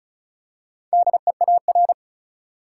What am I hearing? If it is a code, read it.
Morse code, heard as a keyed tone, about 36 wpm, 700 Hz. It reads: DEAR
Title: Four Letter Words - 35wpm